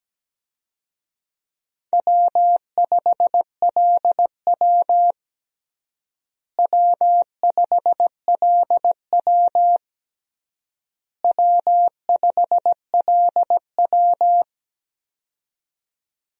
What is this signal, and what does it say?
Morse code, heard as a keyed tone, about 17 wpm, 700 Hz. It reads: W5LW W5LW W5LW
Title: USA Call Signs - 4 Characters - Repeated 3x - 17wpm